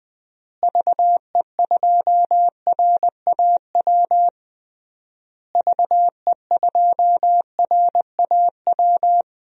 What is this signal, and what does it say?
Morse code, heard as a keyed tone, about 20 wpm, 700 Hz. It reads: VE2RAW VE2RAW
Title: DXCC-100 Call Signs - 6 Characters - Repeated 3x - 20wpm